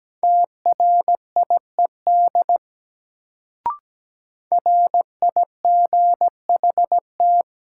Morse code, heard as a keyed tone, about 17 wpm, 700 Hz. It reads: TRIED RIGHT
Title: Sets of 2 Words - New Word Drill - 17wpm